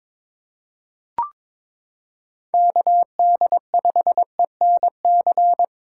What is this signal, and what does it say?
Morse code, heard as a keyed tone, about 22 wpm, 700 Hz. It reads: KD5ENC
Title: USA Callsign DE Callsign - 6 Characters - 22wpm